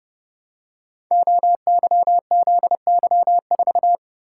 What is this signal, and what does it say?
Morse code, heard as a keyed tone, about 28 wpm, 700 Hz. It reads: OYZY4
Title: Code Groups - 5 Letter:Numbers - 30wpm